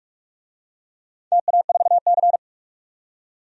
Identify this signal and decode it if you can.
Morse code, heard as a keyed tone, about 43 wpm, 700 Hz. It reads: TA4C
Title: DXCC-100 Call Signs - 4 Characters - 45wpm